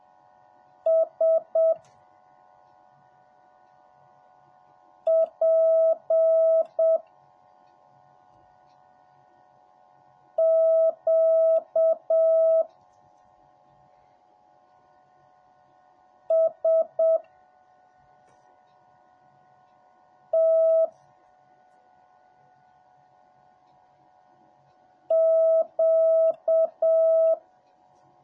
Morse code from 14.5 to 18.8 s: S